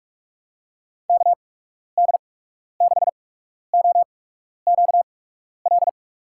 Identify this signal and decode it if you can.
Morse code, heard as a keyed tone, about 45 wpm, spaced as for 12 wpm, 700 Hz. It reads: KD6OQL
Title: USA Call Signs - 6 Characters - 8x Character Spacing - 45wpm